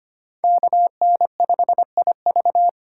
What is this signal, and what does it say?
Morse code, heard as a keyed tone, about 25 wpm, 700 Hz. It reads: KN5IV
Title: Code Groups - 5 Letter:Numbers - 25wpm